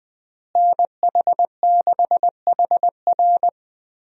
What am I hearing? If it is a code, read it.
Morse code, heard as a keyed tone, about 20 wpm, 700 Hz. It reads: NH6HR
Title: USA Call Signs - 5 Characters - Repeated 3x - 20wpm